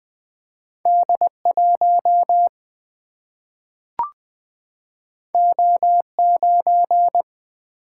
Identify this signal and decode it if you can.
Morse code, heard as a keyed tone, about 20 wpm, 700 Hz. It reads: D 1 O 9
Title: Single-Letter Single-Number 20wpm